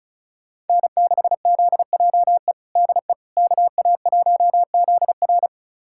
Morse code, heard as a keyed tone, about 34 wpm, 690 Hz. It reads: N6ZJE DE KA1ZR
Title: USA Callsign DE Callsign - 5 Characters - 35wpm